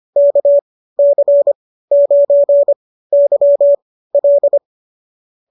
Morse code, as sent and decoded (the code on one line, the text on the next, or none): -.- -.-. ----. -.-- .-..
KC9YL